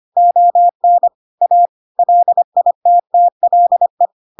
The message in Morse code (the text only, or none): ON A LITTLE